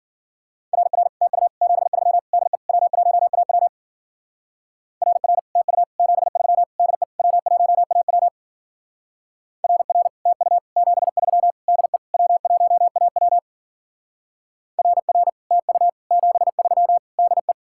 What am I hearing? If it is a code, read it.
Morse code, heard as a keyed tone, about 49 wpm, 700 Hz. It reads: RR TU 73 DE W1AW RR TU 73 DE W1AW RR TU 73 DE W1AW RR TU 73 DE
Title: Speed-Racing Common QSO Phrases - 40wpm